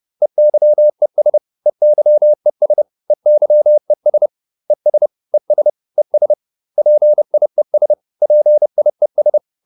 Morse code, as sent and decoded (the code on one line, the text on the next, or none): . -.-- . ... / . -.-- . ... / . -.-- . ... / . ... / . ... / . ... / .--. .. ... / .--. .. ...
EYES EYES EYES ES ES ES PIES PIES